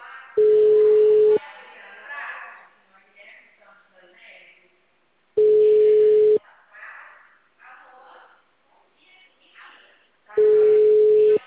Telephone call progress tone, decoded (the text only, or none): ringback tone